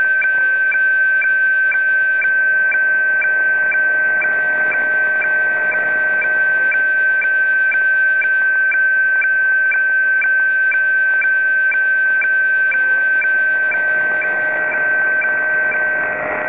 Weather fax (radiofax).
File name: legacy / soundsofhf / hffax-start-phs